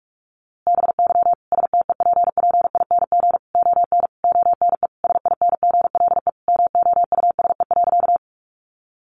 Morse code, heard as a keyed tone, about 44 wpm, 700 Hz. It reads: BY STEPPING ON ONE SINGLE MOUSE.
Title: A Sound Of Thunder 45wpm